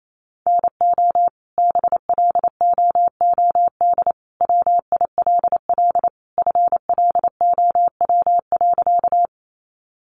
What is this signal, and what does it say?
Morse code, heard as a keyed tone, about 28 wpm, 700 Hz. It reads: NO BLOOD WILL FLOW.
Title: Sentences from Top 1200 Words 28wpm